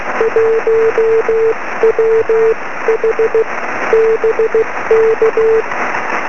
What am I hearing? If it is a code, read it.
Morse code, heard as a keyed tone, about 16 wpm, spaced as for 14 wpm, 470 Hz, with noise. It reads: 1WH BK
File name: sm7lcb / 20 23cm / 1813 pa0ghb 23cm